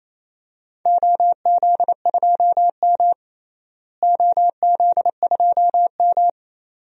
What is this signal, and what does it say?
Morse code, heard as a keyed tone, about 28 wpm, 700 Hz. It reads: OZ2M OZ2M
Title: DXCC-100 Call Signs - 4 Characters - Repeated 2x - 28wpm